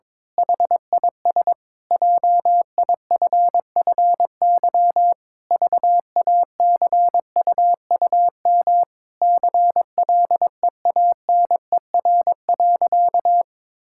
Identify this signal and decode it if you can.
Morse code, heard as a keyed tone, about 21 wpm, 700 Hz. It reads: HIS JIFFY VACUUM CLEANER.
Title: Pangram Challenge v2 - 22wpm